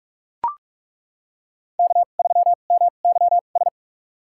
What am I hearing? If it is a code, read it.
Morse code, heard as a keyed tone, about 42 wpm, spaced as for 31 wpm, 700 Hz. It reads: K3MYS
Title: USA Call Signs - 5 Characters - 2x Character Spacing - 45wpm